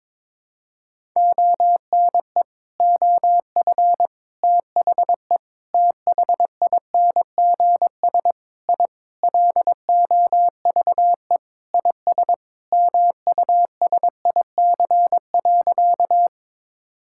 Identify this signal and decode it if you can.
Morse code, heard as a keyed tone, about 22 wpm, 700 Hz. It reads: ONE OF THE THINGS I LOVE IS MUSIC.
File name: practice / 22wpm / Sentences with 4-Word Phrases - 22wpm